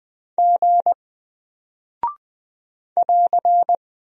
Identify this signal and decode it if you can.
Morse code, heard as a keyed tone, about 20 wpm, 700 Hz. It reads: G <AR>
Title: Single Character (Letter:Number:Punctuation:Prosign) Alt 20wpm